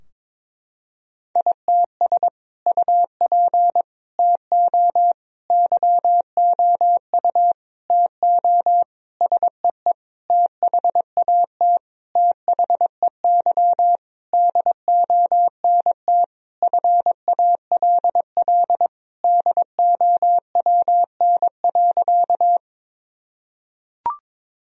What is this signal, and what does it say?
Morse code, heard as a keyed tone, about 22 wpm, 700 Hz. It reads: ITS UP TO YOU TO SEE THAT THEY DONT FALL DOWN.
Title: Sentences from Top 1000 Words - Review Part 2 - 22wpm